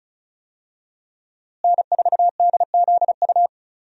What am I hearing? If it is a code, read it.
Morse code, heard as a keyed tone, about 36 wpm, 700 Hz. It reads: N4DZU
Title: USA Call Signs - 5 Characters - Repeated 2x - 35wpm